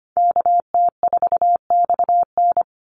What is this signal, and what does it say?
Morse code, heard as a keyed tone, about 25 wpm, 700 Hz. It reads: KT4XN
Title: Call Signs Repeated 3x - 25wpm